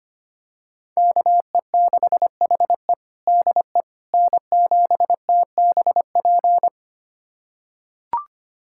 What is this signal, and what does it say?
Morse code, heard as a keyed tone, about 25 wpm, 700 Hz. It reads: KE6HE DE N7TBP
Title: USA Callsign DE Callsign - 5 Characters - 25wpm